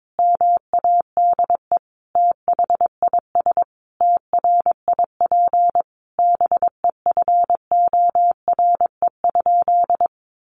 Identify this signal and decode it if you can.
Morse code, heard as a keyed tone, about 22 wpm, 700 Hz. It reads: MADE THIS TRIP BEFORE?
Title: Sentences from Top 1300 Words 22wpm